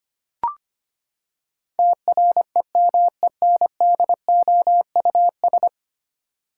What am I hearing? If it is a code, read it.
Morse code, heard as a keyed tone, about 25 wpm, 700 Hz. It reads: TREMENDOUS
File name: Ten Letter Words - 25wpm